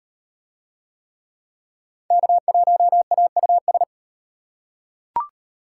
Morse code, heard as a keyed tone, about 37 wpm, 700 Hz. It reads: K1AUS